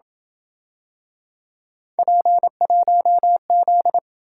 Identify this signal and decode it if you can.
Morse code, heard as a keyed tone, about 26 wpm, 700 Hz. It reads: P1Z